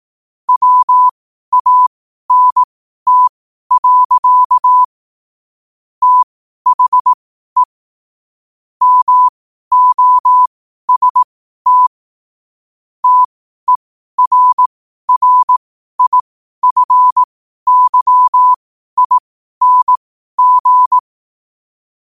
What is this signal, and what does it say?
Morse code, heard as a keyed tone, about 18 wpm, spaced as for 12 wpm, 1000 Hz. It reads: WANT. THE MOST TERRIFYING